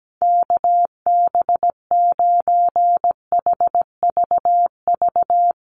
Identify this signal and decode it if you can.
Morse code, heard as a keyed tone, about 17 wpm, 700 Hz. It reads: KB9HVV
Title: USA Call Signs Repeated 3x 17wpm